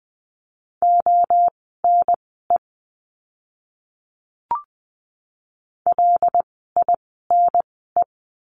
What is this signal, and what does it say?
Morse code, heard as a keyed tone, about 20 wpm, spaced as for 15 wpm, 700 Hz. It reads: ONE LINE
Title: 115 Lesson 19f - Letter w - Words 20wpm